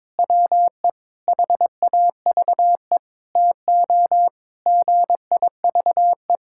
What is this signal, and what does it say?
Morse code, heard as a keyed tone, about 22 wpm, 700 Hz. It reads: WE HAVE TO GIVE
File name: practice / 22wpm / Sentences with 4-Word Phrases - 22wpm